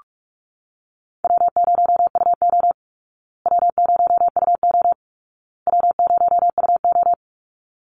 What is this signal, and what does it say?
Morse code, heard as a keyed tone, about 44 wpm, 700 Hz. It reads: W0UO W0UO W0UO
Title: Call Signs Repeated 2x - 45wpm